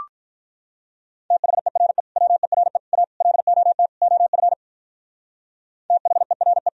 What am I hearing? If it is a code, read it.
Morse code, heard as a keyed tone, about 42 wpm, 700 Hz. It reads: THERE WERE A LOT OF THERE
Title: Speed-Racing - Sets of 5 Words 35wpm